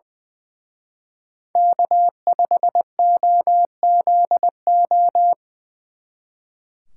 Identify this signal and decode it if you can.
Morse code, heard as a keyed tone, about 24 wpm, 700 Hz. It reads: K5OZO
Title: Speed Racing - USA 1x3 Call Signs 20wpm